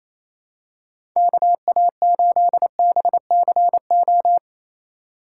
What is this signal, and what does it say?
Morse code, heard as a keyed tone, about 28 wpm, 700 Hz. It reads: KA8BCO
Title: USA Call Signs - 6 Characters - 2x Character Spacing - 28wpm